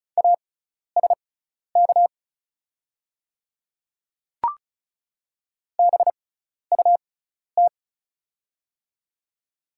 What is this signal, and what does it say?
Morse code, heard as a keyed tone, about 36 wpm, spaced as for 12 wpm, 700 Hz. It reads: ASK BUT
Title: 6x Character Spacing - Top 200 Words - 35wpm